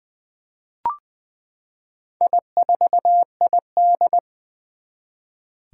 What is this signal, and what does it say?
Morse code, heard as a keyed tone, about 20 wpm, 700 Hz. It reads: I4ID
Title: Four Challenging Letter-Number Combinations 20wpm